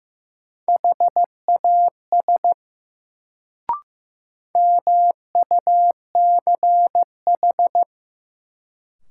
Morse code, heard as a keyed tone, about 15 wpm, 700 Hz. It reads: HAS MUCH
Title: Sets of 2 Words - New Word Drill - 15wpm